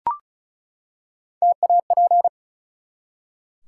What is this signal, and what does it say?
Morse code, heard as a keyed tone, about 36 wpm, 700 Hz. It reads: TAP
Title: Three Letter Words - 35wpm